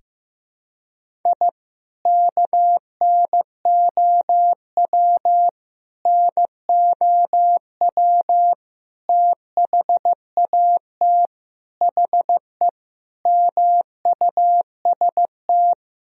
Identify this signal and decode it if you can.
Morse code, heard as a keyed tone, about 15 wpm, 700 Hz. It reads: I KNOW NOW THAT HE MUST
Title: Sentences from Top 1000 Words - Review Part 1 - 15wpm